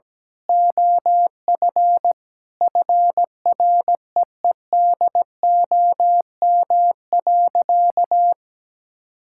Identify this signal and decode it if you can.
Morse code, heard as a keyed tone, about 17 wpm, 700 Hz. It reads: OF FREEDOM.
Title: Sentences from Top 1600 Words 17wpm